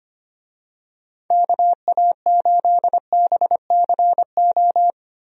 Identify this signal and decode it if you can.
Morse code, heard as a keyed tone, about 25 wpm, 700 Hz. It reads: KA8BCO
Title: USA Call Signs - 6 Characters - 25wpm